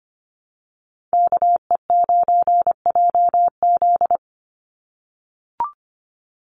Morse code, heard as a keyed tone, about 25 wpm, 700 Hz. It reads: KE9JZ
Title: USA Call Signs 25wpm